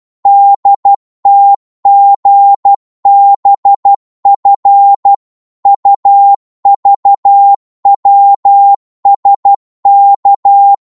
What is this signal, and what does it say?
Morse code, heard as a keyed tone, about 12 wpm, 800 Hz. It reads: DTGBF UVWSK
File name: Morse / ltr-12wpm